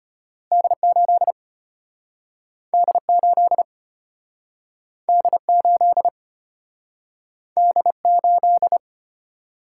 Morse code, heard as a keyed tone, about 34 wpm, 700 Hz. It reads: D8 D8 D8 D8